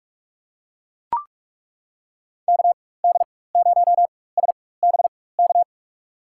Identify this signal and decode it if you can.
Morse code, heard as a keyed tone, about 42 wpm, spaced as for 20 wpm, 700 Hz. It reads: KD0SBK
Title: USA Call Signs - 4x Character Spacing - 45wpm